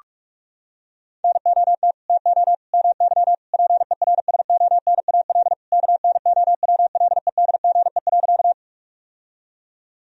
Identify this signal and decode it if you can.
Morse code, heard as a keyed tone, about 41 wpm, 700 Hz. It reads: NOT TO MY PERSONAL KNOWLEDGE.
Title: Sentences from Top 1000 Words - Review Part 4 - 45wpm